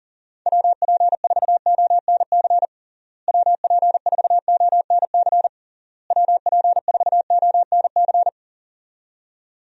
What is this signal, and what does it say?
Morse code, heard as a keyed tone, about 40 wpm, 700 Hz. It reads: WP4ONC WP4ONC WP4ONC
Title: USA Call Signs - 6 Characters - Repeated 3x - 40wpm